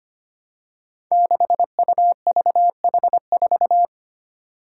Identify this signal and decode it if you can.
Morse code, heard as a keyed tone, about 25 wpm, 700 Hz. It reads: 6UVH4
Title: Five Challenging Letter-Number Combinations 25wpm